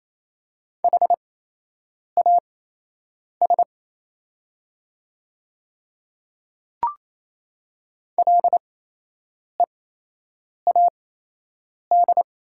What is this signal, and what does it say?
Morse code, heard as a keyed tone, about 28 wpm, spaced as for 8 wpm, 700 Hz. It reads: HAS LEAD
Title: 8x Character Spacing - Top 500 Words - 28wpm